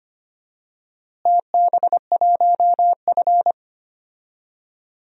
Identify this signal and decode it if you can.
Morse code, heard as a keyed tone, about 25 wpm, 700 Hz. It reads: TB1F
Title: DXCC-100 Call Signs - 4 Characters - Repeated 2x - 25wpm